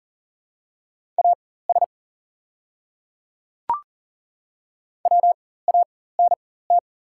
Morse code, heard as a keyed tone, about 40 wpm, spaced as for 19 wpm, 700 Hz. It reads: AS WANT
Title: 4x Character Spacing - Top 300 Words - 40wpm